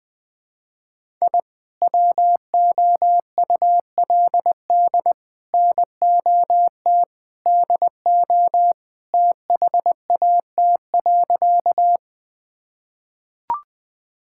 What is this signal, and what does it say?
Morse code, heard as a keyed tone, about 20 wpm, 700 Hz. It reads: I WOULD NOT DO THAT.
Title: Sentences with 3-Word Phrases - 20wpm